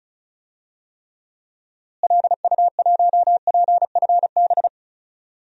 Morse code, heard as a keyed tone, about 36 wpm, 700 Hz. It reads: LU1PFB
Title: DXCC-100 Call Signs - 6 Characters - Repeated 2x - 35wpm